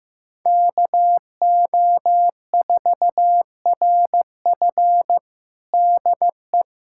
Morse code, heard as a keyed tone, about 15 wpm, 700 Hz. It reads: KO4RF DE
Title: USA Callsign DE Callsign - 5 Characters - 15wpm